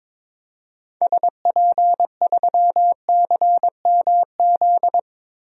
Morse code, heard as a keyed tone, about 22 wpm, 700 Hz. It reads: SP3CMZ
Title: DXCC-100 Call Signs - 6 Characters - Repeated 2x - 22wpm